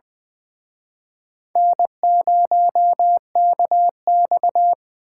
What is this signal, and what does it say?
Morse code, heard as a keyed tone, about 23 wpm, 700 Hz. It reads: N0KX
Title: Speed Racing - USA 1x2 Call Signs 20wpm